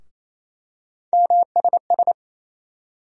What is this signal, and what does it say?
Morse code, heard as a keyed tone, about 28 wpm, 700 Hz. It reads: MSS
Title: USA Call Sign Postfixes - 3 Letters 28wpm